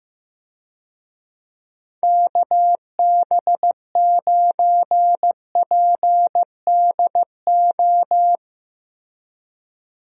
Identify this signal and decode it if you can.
Morse code, heard as a keyed tone, about 15 wpm, 700 Hz. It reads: KB9PDO
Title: USA Call Signs - 6 Characters - Repeated 3x - 15wpm